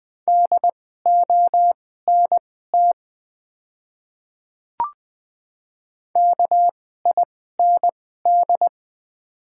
Morse code, heard as a keyed tone, about 20 wpm, spaced as for 15 wpm, 700 Hz. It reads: DONT KIND
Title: 2x Character Spacing - Top 300 Words - 20wpm